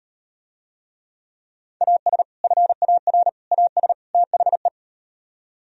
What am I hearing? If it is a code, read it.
Morse code, heard as a keyed tone, about 38 wpm, 700 Hz. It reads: AS FAR AS THE